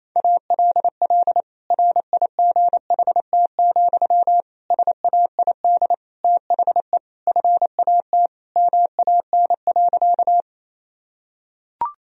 Morse code, heard as a keyed tone, about 28 wpm, 700 Hz. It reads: ALL RIGHT, SAID THE FAT MAN.
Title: Sentences from Top 1000 Words - Review Part 4 - 28wpm